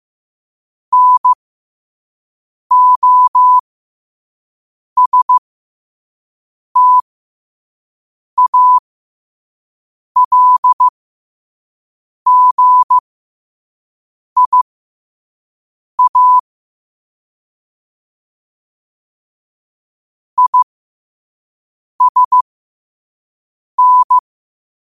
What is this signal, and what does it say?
Morse code, heard as a keyed tone, about 15 wpm, spaced as for 5 wpm, 1000 Hz. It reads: NOSTALGIA ISN